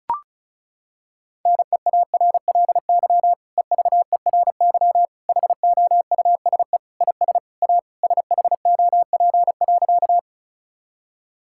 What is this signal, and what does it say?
Morse code, heard as a keyed tone, about 36 wpm, 700 Hz. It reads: NEARLY EVERY HOUSE IS A SHOP.
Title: Sentences from Top 1600 Words 35wpm